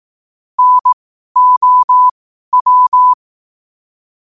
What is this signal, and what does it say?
Morse code, heard as a keyed tone, about 18 wpm, spaced as for 12 wpm, 1000 Hz. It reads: NOW
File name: qotdfiles / 12wpm